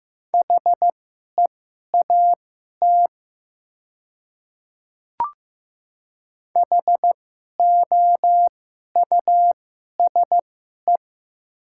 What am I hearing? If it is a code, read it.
Morse code, heard as a keyed tone, about 15 wpm, spaced as for 11 wpm, 700 Hz. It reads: HEAT HOUSE
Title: 2x Character Spacing - Top 500 Words - 15wpm